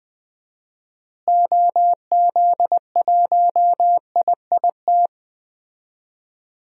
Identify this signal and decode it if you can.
Morse code, heard as a keyed tone, about 20 wpm, 700 Hz. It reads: OZ1IIT
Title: DXCC-100 Call Signs - 6 Characters - Repeated 3x - 20wpm